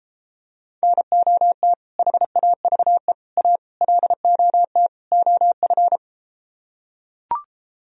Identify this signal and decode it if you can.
Morse code, heard as a keyed tone, about 34 wpm, 700 Hz. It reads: NOT HAVE A LOT OF